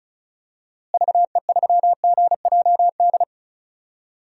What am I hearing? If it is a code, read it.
Morse code, heard as a keyed tone, about 34 wpm, 700 Hz. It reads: VE3GJD
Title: DXCC-100 Call Signs - 6 Characters - Repeated 2x - 35wpm